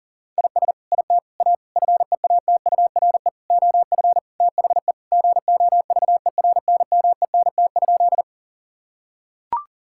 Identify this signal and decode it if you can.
Morse code, heard as a keyed tone, about 37 wpm, 700 Hz. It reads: IS IT A FEATURE OF THE GOVERNMENT?
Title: Sentences from Top 1000 Words - Review Part 5 - 40wpm